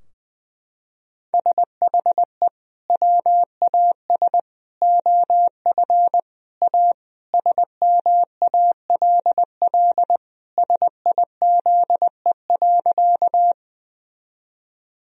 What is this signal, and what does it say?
Morse code, heard as a keyed tone, about 20 wpm, 700 Hz. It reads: SHE WAS OF A SMALL SIZE.
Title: Sentences from Top 1000 Words - Review Part 2 - 20wpm